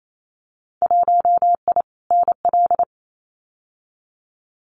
Morse code, heard as a keyed tone, about 28 wpm, 700 Hz. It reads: 1I NL
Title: Winter Field Day Exchanges 28wpm